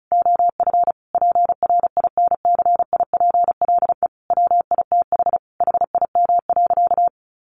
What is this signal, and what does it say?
Morse code, heard as a keyed tone, about 34 wpm, 700 Hz. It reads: OF PRINCIPLE WITH HIM.
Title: Sentences from Top 1400 Words 35wpm